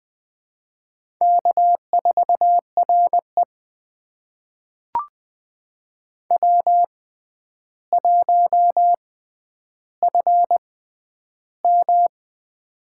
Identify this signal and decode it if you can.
Morse code, heard as a keyed tone, about 20 wpm, spaced as for 10 wpm, 700 Hz. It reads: K4RE W1FM
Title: USA Call Signs - 4 Characters - 6x Character Spacing - 20wpm